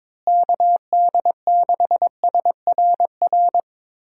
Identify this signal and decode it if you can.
Morse code, heard as a keyed tone, about 21 wpm, 700 Hz. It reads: KD6SRR
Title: USA Call Signs - 6 Characters - Repeated 3x - 22wpm